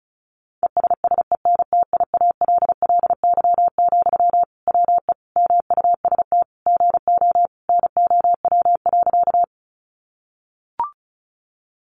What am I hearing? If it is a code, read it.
Morse code, heard as a keyed tone, about 34 wpm, 700 Hz. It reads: ESSENTIALLY, WE MUST GO NOW.
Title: Sentences from Top 1300 Words 35wpm